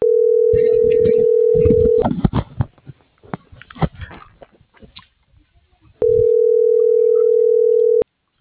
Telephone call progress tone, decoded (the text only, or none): ringback tone